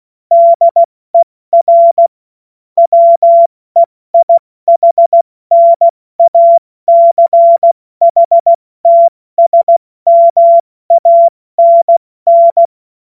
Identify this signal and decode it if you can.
Morse code, heard as a keyed tone, about 16 wpm, 680 Hz. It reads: DER WEIHNACHTSMANN
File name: Morsekurs / Witze / Witz 18. April 70 F80